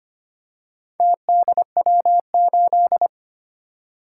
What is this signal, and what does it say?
Morse code, heard as a keyed tone, about 25 wpm, 700 Hz. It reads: TDW8
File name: Four Letter-Number Combinations - 25wpm